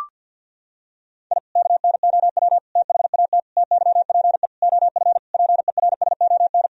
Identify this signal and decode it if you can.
Morse code, heard as a keyed tone, about 51 wpm, 700 Hz. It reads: I KNOW THAT TYPE OF PERSON
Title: Sentences from Top 1000 Words - Review Part 3 - 50wpm